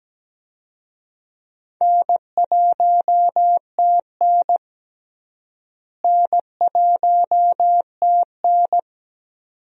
Morse code, heard as a keyed tone, about 17 wpm, 700 Hz. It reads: N1TN N1TN
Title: USA Call Signs - 4 Characters - Repeated 2x - 17wpm